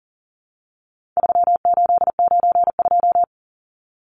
Morse code, heard as a keyed tone, about 40 wpm, 700 Hz. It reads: 3892
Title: Numbers - 4-Digit 40wpm